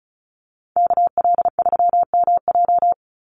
Morse code, heard as a keyed tone, about 34 wpm, 700 Hz. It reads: KL3MJ
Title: USA Call Signs 35wpm